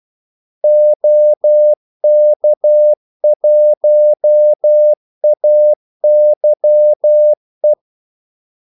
Morse code, heard as a keyed tone, about 12 wpm, 600 Hz. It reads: OK1AYE
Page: 2 ekstra ordmellemrum mellem kaldesignalerne